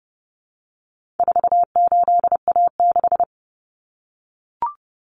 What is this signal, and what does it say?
Morse code, heard as a keyed tone, about 30 wpm, 700 Hz. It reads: 48A6
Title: Common Cut Numbers - 4-Digit 30wpm